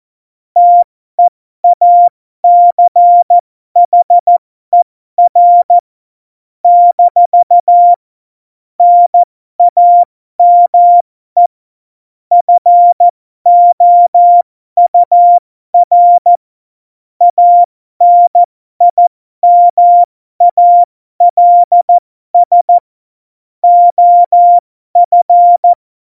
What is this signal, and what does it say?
Morse code, heard as a keyed tone, about 14 wpm, spaced as for 12 wpm, 700 Hz. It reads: TEACHER - NAME FOUR ANIMALS OF
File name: Morsekurs / Witze